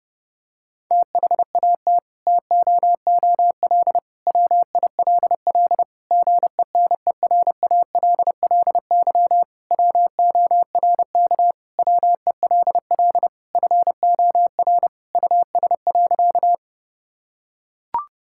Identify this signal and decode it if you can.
Morse code, heard as a keyed tone, about 32 wpm, 700 Hz. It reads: THAT TOOL WILL GENERALLY WORK WELL FOR US.
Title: Sentences from Top 1000 Words - Review Part 6 - 30wpm